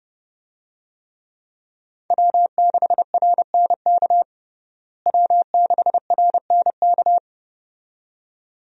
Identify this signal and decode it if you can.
Morse code, heard as a keyed tone, about 32 wpm, 700 Hz. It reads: W6RNK W6RNK
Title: USA Call Signs - 5 Characters - Repeated 2x - 30wpm